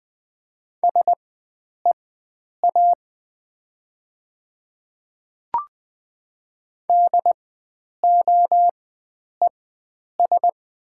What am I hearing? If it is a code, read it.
Morse code, heard as a keyed tone, about 20 wpm, spaced as for 9 wpm, 700 Hz. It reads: SEA DOES
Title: 4x Character Spacing - Top 300 Words - 20wpm